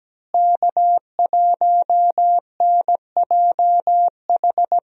Morse code, heard as a keyed tone, about 17 wpm, 700 Hz. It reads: K1NJH
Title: USA Call Signs - 5 Characters - Repeated 3x - 17wpm